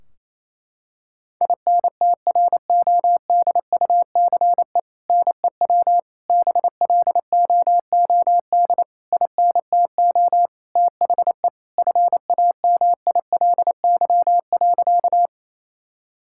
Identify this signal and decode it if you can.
Morse code, heard as a keyed tone, about 28 wpm, 700 Hz. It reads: INTRODUCE NEW BLOOD INTO THE FAMILY.
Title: Sentences from Top 1600 Words 28wpm